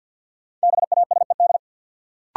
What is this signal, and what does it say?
Morse code, heard as a keyed tone, about 53 wpm, 700 Hz. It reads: BASED
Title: Sets of 2 Words - New Word Drill - 50wpm